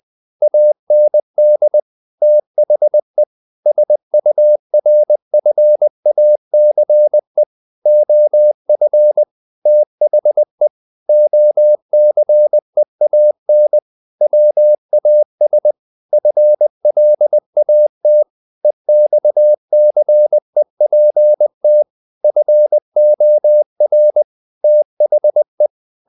Morse code, heard as a keyed tone, about 20 wpm, 600 Hz. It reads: AND THE SURFACE OF THE OCEAN WAS FLAT EXCEPT FOR THE
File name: the old man and the sea / the old man and the sea part 1 20wpm